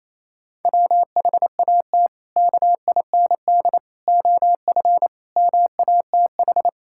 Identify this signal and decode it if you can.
Morse code, heard as a keyed tone, about 27 wpm, 700 Hz. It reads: WHAT KIND OF MATH
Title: Sentences from Top 1000 Words - Review Part 6 - 28wpm